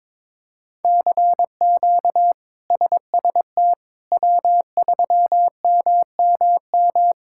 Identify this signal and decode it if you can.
Morse code, heard as a keyed tone, about 22 wpm, 700 Hz. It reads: CQ SST W3MMM
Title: K1USN SST QSO Practice - 22wpm